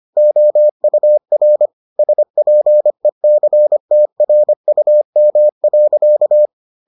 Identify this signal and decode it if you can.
Morse code, heard as a keyed tone, about 25 wpm, 600 Hz. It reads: OUR SPECTRUM.